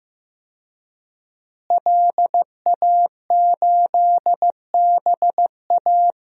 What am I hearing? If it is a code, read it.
Morse code, heard as a keyed tone, about 15 wpm, 700 Hz. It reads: LA8BA
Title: DXCC-100 Call Signs - 5 Characters - Repeated 2x - 15wpm